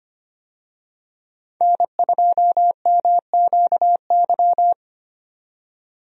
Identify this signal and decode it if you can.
Morse code, heard as a keyed tone, about 25 wpm, 700 Hz. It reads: N2MQY
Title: USA Call Signs - 5 Characters - 25wpm